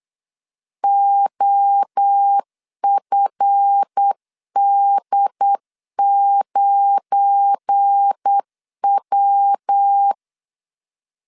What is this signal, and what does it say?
Morse code, heard as a keyed tone, about 8 wpm, 800 Hz. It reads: OFD9W
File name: LEZIONI / 7 GF9Z / 7 GF9 8wpm 3545